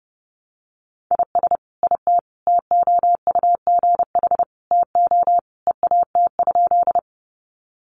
Morse code, heard as a keyed tone, about 30 wpm, 700 Hz. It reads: IS IT TOUGH TO EAT?
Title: Sentences from Top 1200 Words 30wpm